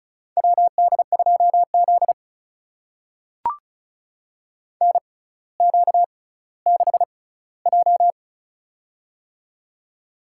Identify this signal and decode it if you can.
Morse code, heard as a keyed tone, about 34 wpm, spaced as for 18 wpm, 700 Hz. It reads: WD2Z NQ6J
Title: USA Call Signs - 4 Characters - 6x Character Spacing - 35wpm